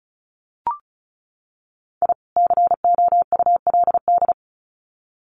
Morse code, heard as a keyed tone, about 34 wpm, 700 Hz. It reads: I COULD
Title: Sets of 2 Words 35wpm